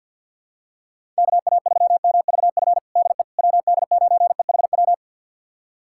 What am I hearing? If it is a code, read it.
Morse code, heard as a keyed tone, about 51 wpm, 700 Hz. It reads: KA3MVF DE WD9EHW
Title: USA Callsign DE Callsign 50wpm